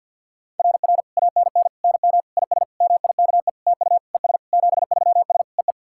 Morse code, heard as a keyed tone, about 49 wpm, 690 Hz. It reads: RR ANN NM IS MIKE TU ES 73S EE
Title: SOTA QSO Practice 50wpm